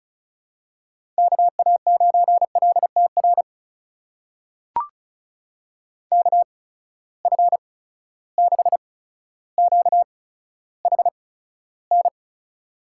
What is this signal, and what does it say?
Morse code, heard as a keyed tone, about 36 wpm, spaced as for 15 wpm, 700 Hz. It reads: KA9LTR KF6QHN
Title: USA Call Signs - 6 Characters - 8x Character Spacing - 35wpm